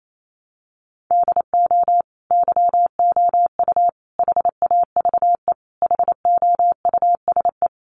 Morse code, heard as a keyed tone, about 28 wpm, 700 Hz. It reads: DO YOU HAVE HOUSE
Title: Sentences from Top 1100 Words 28wpm